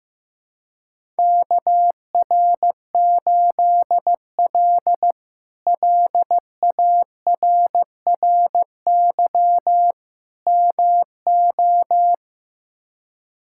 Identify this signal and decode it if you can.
Morse code, heard as a keyed tone, about 15 wpm, 700 Hz. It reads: KR8L LARRY MO
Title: K1USN SST QSO Practice - 15wpm